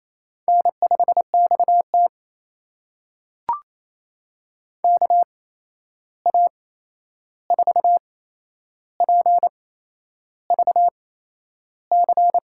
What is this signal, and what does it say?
Morse code, heard as a keyed tone, about 27 wpm, spaced as for 10 wpm, 700 Hz. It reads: N5XT KA4PVC
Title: USA Call Signs - 8x Character Spacing - 28wpm